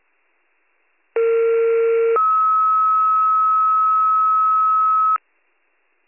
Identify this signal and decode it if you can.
Two-tone page, heard as a one-second tone, then a three-second tone, about 460 then 1290 Hz. It is heard from 1.2 to 5.2 s.